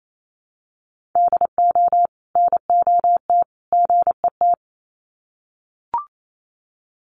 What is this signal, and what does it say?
Morse code, heard as a keyed tone, about 28 wpm, 700 Hz. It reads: DO NOT GET
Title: Sets of 3 Words 28wpm